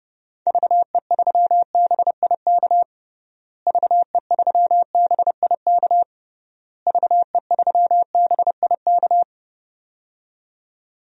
Morse code, heard as a keyed tone, about 30 wpm, 700 Hz. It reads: VE3BIK VE3BIK VE3BIK
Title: DXCC-100 Call Signs - 6 Characters - Repeated 3x - 30wpm